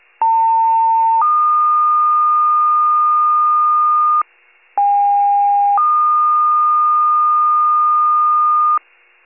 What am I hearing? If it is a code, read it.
Two-tone page, heard as a one-second tone, then a three-second tone, about 900 then 1220 Hz, about 810 then 1220 Hz. It is heard from 0.2 to 8.8 s.